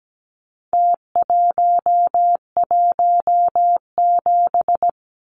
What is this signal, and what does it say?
Morse code, heard as a keyed tone, about 17 wpm, 700 Hz. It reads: T117
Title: Numbers - Contest Style - 4-Digit 17wpm